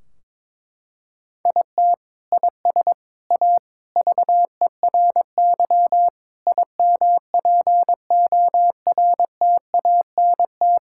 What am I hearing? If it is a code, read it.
Morse code, heard as a keyed tone, about 22 wpm, 700 Hz. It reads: IT IS A VERY IMPORTANT
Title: Sentences from Top 1600 Words 22wpm